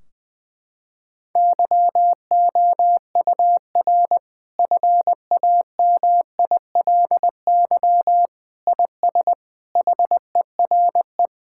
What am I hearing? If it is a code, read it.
Morse code, heard as a keyed tone, about 21 wpm, 700 Hz. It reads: YOUR FAMILY IS HERE
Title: Sentences from Top 1600 Words 20wpm